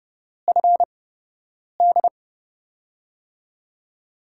Morse code, heard as a keyed tone, about 30 wpm, spaced as for 8 wpm, 700 Hz. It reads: FD
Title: USA Call Signs - 5 Characters - 8x Character Spacing - 30wpm